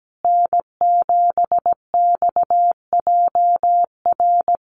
Morse code, heard as a keyed tone, about 17 wpm, 700 Hz. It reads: N7XJR
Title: USA Call Signs Repeated 3x 17wpm